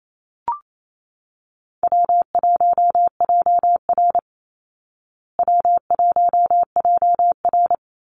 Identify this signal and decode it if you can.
Morse code, heard as a keyed tone, about 28 wpm, 700 Hz. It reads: W1JR W1JR
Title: Call Signs Repeated 2x - 28wpm